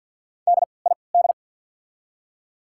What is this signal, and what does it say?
Morse code, heard as a keyed tone, about 53 wpm, spaced as for 29 wpm, 700 Hz. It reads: DID
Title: 3x Character Spacing - Top 300 Words - 50wpm